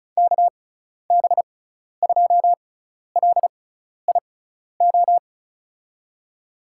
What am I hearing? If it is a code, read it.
Morse code, heard as a keyed tone, about 34 wpm, spaced as for 12 wpm, 700 Hz. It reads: KB2LIO
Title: USA Call Signs - 6 Characters - 6x Character Spacing - 35wpm